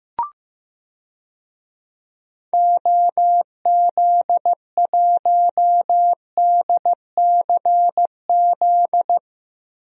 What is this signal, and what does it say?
Morse code, heard as a keyed tone, about 15 wpm, 700 Hz. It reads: OZ1DCZ
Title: DXCC-100 Call Signs - 6 Characters - Repeated 2x - 15wpm